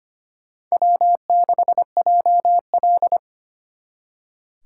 Morse code, heard as a keyed tone, about 32 wpm, 700 Hz. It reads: W6JL